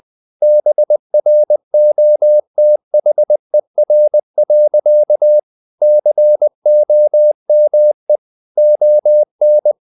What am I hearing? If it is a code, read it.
Morse code, heard as a keyed tone, about 20 wpm, 600 Hz. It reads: BROTHER. COME ON
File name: the old man and the sea part 2 20wpm